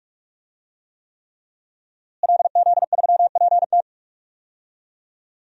Morse code, heard as a keyed tone, about 41 wpm, 700 Hz. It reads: LZ3PT
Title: DXCC-100 Call Signs - 5 Characters - Repeated 3x - 45wpm